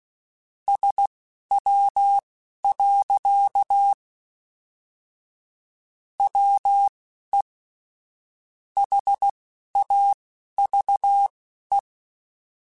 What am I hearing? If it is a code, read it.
Morse code, heard as a keyed tone, about 16 wpm, spaced as for 11 wpm, 800 Hz. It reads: SW. WE HAVE
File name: Weather Sets / Weather Sets-mp3s / Weather 5 5 @10